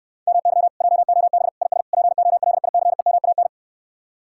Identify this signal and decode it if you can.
Morse code, heard as a keyed tone, about 69 wpm, 700 Hz. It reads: MY JOB IS POLICEMAN